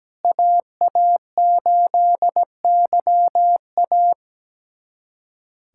Morse code, heard as a keyed tone, about 17 wpm, 700 Hz. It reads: AA8YA
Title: USA Callsign DE Callsign - 5 Characters - 17wpm